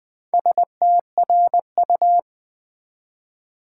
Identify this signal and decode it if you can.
Morse code, heard as a keyed tone, about 20 wpm, 700 Hz. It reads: STRU
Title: Build A Word - 2 Letters At A Time - 8-Letter Words - 20wpm